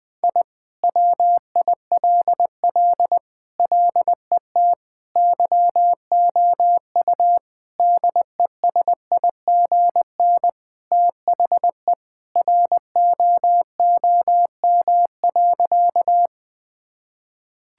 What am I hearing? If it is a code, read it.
Morse code, heard as a keyed tone, about 20 wpm, 700 Hz. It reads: I WILL LET YOU DESIGN THE ROOM.
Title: Sentences from Top 1000 Words - Review Part 5 - 20wpm